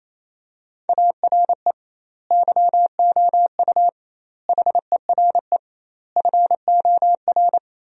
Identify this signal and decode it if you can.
Morse code, heard as a keyed tone, about 28 wpm, 700 Hz. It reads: ARE YOU HERE FOR
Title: 2x Word-Spacing - Sentences from Top 200 Words - 28wpm